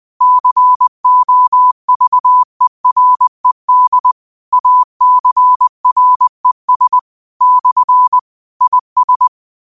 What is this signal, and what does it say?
Morse code, heard as a keyed tone, about 20 wpm, 1000 Hz. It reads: COVERED ACRES / IS
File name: qotdfiles / 20wpm